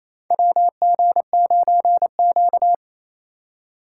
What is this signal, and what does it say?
Morse code, heard as a keyed tone, about 28 wpm, 700 Hz. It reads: WG9Q
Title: USA Call Signs - 4 Characters - 4x Character Spacing - 28wpm